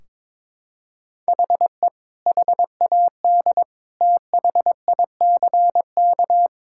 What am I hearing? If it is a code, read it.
Morse code, heard as a keyed tone, about 21 wpm, 700 Hz. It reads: HE HAD THICK
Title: Sentences from Top 1600 Words 22wpm